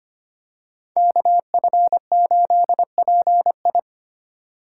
Morse code, heard as a keyed tone, about 25 wpm, 700 Hz. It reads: KF8PI
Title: USA Call Signs - 5 Characters - 3x Character Spacing - 25wpm